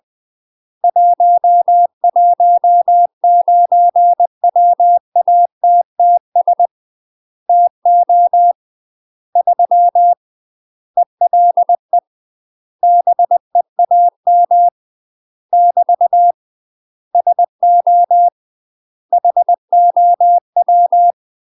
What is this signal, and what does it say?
Morse code, heard as a keyed tone, about 20 wpm, 700 Hz. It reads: 119 WATTS TO 3 ELE BEAM <BT> SO HOW
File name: UK MASTER QSOs TRAINING / 20WPM / UK MASTER QSOs 0028